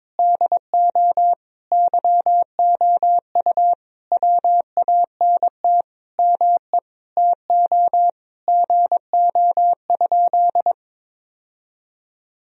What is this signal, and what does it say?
Morse code, heard as a keyed tone, about 22 wpm, 700 Hz. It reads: DO YOU WANT ME TO GO?
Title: Sentences from Top 1000 Words - Review Part 3 - 22wpm